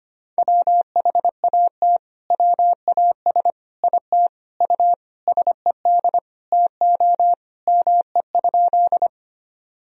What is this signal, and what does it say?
Morse code, heard as a keyed tone, about 25 wpm, 700 Hz. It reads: WHAT WAS IT U SED TO ME?
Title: Sentences from Top 200 Words and Abbreviations 25wpm